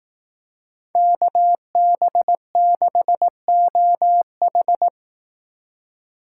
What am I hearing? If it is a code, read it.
Morse code, heard as a keyed tone, about 18 wpm, 700 Hz. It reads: KB6OH
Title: Speed Racing - USA 2x2 Call Signs 15wpm